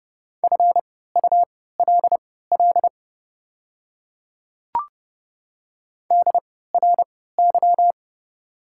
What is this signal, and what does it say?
Morse code, heard as a keyed tone, about 30 wpm, spaced as for 17 wpm, 690 Hz. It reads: FULL DRY
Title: 3x Character Spacing - Top 500 Words - 30wpm